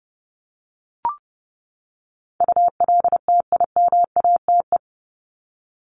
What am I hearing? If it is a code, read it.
Morse code, heard as a keyed tone, about 30 wpm, 700 Hz. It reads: ULTIMATE
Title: LY Suffix - Top 100 Words with Base Words 30wpm